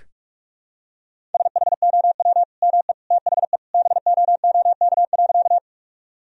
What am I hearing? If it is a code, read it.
Morse code, heard as a keyed tone, about 41 wpm, 700 Hz. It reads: SHOW ME THE BOOK.
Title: Sentences from Top 1000 Words - Review Part 1 - 45wpm